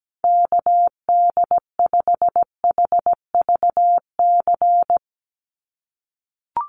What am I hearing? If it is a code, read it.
Morse code, heard as a keyed tone, about 17 wpm, 700 Hz. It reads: KD5HVC
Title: USA Call Signs 17wpm